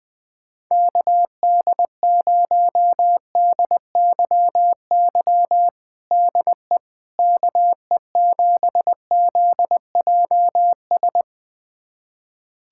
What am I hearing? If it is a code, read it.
Morse code, heard as a keyed tone, about 20 wpm, 700 Hz. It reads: KD0DYY DE KE7ZJS
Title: USA Callsign DE Callsign - 6 Characters - 20wpm